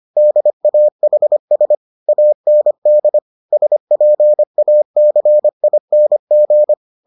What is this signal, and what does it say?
Morse code, heard as a keyed tone, about 25 wpm, 600 Hz. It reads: DAHS AND SPACING